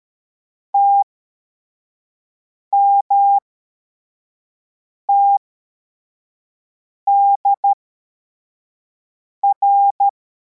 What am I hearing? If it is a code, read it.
Morse code, heard as a keyed tone, about 13 wpm, spaced as for 4 wpm, 800 Hz. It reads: TMTDR
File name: formation / cw / Text25